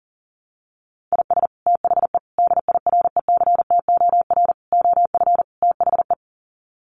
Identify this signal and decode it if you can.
Morse code, heard as a keyed tone, about 37 wpm, 700 Hz. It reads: IS THE DIRECTOR OF THE
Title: Sets of 5 Words Encore2 40wpm